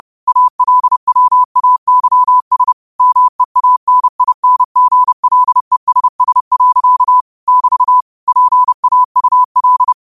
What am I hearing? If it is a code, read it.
Morse code, heard as a keyed tone, about 30 wpm, 1000 Hz. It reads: ALWAYS MEANINGLESS. <BT> PAUL
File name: qotdfiles / 30wpm